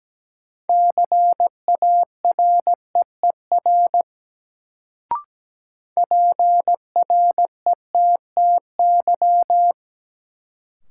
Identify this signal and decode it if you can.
Morse code, heard as a keyed tone, about 17 wpm, 700 Hz. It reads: CAREER PRETTY
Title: Sentences from Top 1000 Words - New Word Drill - 17wpm